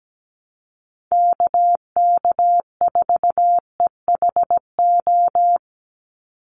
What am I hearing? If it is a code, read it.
Morse code, heard as a keyed tone, about 17 wpm, 700 Hz. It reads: KK4EHO
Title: USA Call Signs 17wpm